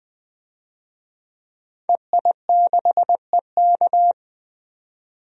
Morse code, heard as a keyed tone, about 20 wpm, 700 Hz. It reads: EI6EK
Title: DXCC-100 Call Signs - 5 Characters - 20wpm